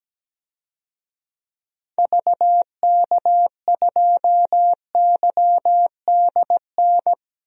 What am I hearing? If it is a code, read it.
Morse code, heard as a keyed tone, about 17 wpm, 700 Hz. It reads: VK2YDN
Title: DXCC-100 Call Signs - 6 Characters - 17wpm